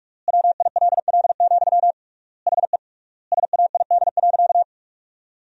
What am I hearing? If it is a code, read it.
Morse code, heard as a keyed tone, about 47 wpm, 700 Hz. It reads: WILL, HE SAID.
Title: 3x Word-Spacing - Sentences from Top 200 Words - 45wpm